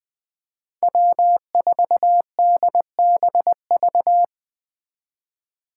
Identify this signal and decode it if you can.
Morse code, heard as a keyed tone, about 25 wpm, 700 Hz. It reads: W4DBV
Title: Speed Racing - USA 1x3 Call Signs 20wpm